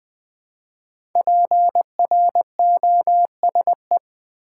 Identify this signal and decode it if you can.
Morse code, heard as a keyed tone, about 20 wpm, 700 Hz. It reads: PROSE
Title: Build A Word - 5 Letters At A Time - 10-Letter Words - 20wpm